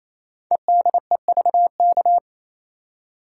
Morse code, heard as a keyed tone, about 27 wpm, 700 Hz. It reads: EDEVK
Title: Code Groups - 5 Letters - 28wpm